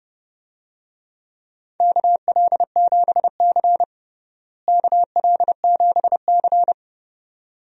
Morse code, heard as a keyed tone, about 30 wpm, 700 Hz. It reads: KL7C KL7C
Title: DXCC-100 Call Signs - 4 Characters - Repeated 2x - 30wpm